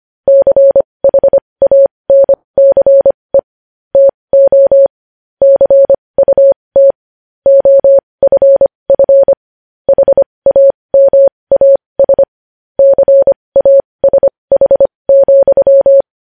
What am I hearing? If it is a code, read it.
Morse code, heard as a keyed tone, about 25 wpm, spaced as for 20 wpm, 550 Hz. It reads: CHANCE TO CUT OFF HAMAS CASH,
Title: News Headlines at 20